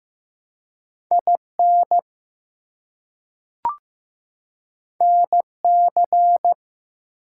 Morse code, heard as a keyed tone, about 15 wpm, 700 Hz. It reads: IN NC
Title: US States and Canadian Provinces 15wpm